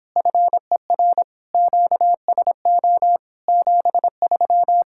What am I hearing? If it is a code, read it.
Morse code, heard as a keyed tone, about 26 wpm, 700 Hz. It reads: FER QSO 73